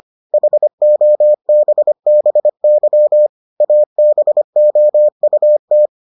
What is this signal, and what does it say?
Morse code, heard as a keyed tone, about 25 wpm, 600 Hz. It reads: HOBBY ABOUT